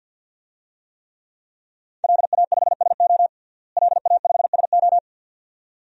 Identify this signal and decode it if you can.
Morse code, heard as a keyed tone, about 46 wpm, 700 Hz. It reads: LA5SO LA5SO
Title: DXCC-100 Call Signs - 5 Characters - Repeated 2x - 50wpm